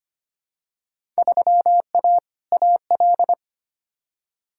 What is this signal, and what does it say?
Morse code, heard as a keyed tone, about 25 wpm, 700 Hz. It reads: 3A AL
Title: Field Day Exchanges - 25wpm